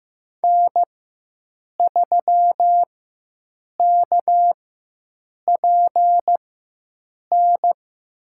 Morse code, heard as a keyed tone, about 15 wpm, spaced as for 7 wpm, 700 Hz. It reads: N3KPN
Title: USA Call Signs - 5 Characters - 4x Character Spacing - 15wpm